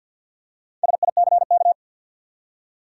Morse code, heard as a keyed tone, about 51 wpm, 700 Hz. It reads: SICK
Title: Four Letter Words - 50wpm